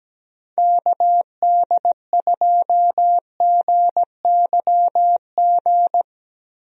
Morse code, heard as a keyed tone, about 17 wpm, 700 Hz. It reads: KD2GYG
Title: USA Call Signs - 6 Characters - Repeated 2x - 17wpm